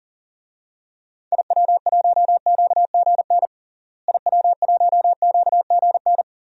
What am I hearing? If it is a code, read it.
Morse code, heard as a keyed tone, about 40 wpm, 700 Hz. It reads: IW1QGN IW1QGN
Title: DXCC-100 Call Signs - 6 Characters - Repeated 2x - 40wpm